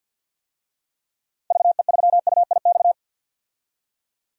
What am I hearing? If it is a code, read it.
Morse code, heard as a keyed tone, about 49 wpm, 700 Hz. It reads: VE3UIX
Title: DXCC-100 Call Signs - 6 Characters - 50wpm